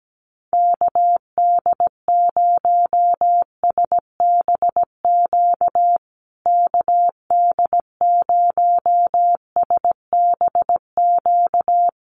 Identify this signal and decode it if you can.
Morse code, heard as a keyed tone, about 17 wpm, 700 Hz. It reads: KD0SBQ KD0SBQ
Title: USA Call Signs Repeated 3x 17wpm